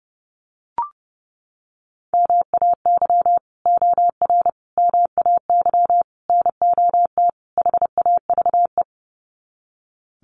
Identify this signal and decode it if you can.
Morse code, heard as a keyed tone, about 30 wpm, 700 Hz. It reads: MAY OR MAY NOT HAVE
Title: Sets of 5 Words Encore2 30wpm